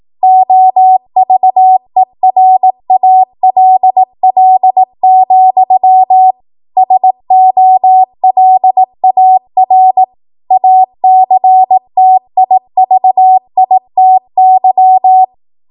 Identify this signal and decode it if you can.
Morse code, heard as a keyed tone, about 18 wpm, 750 Hz. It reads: OVERALL, SOLAR ACTIVITY
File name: Morse / 18WPM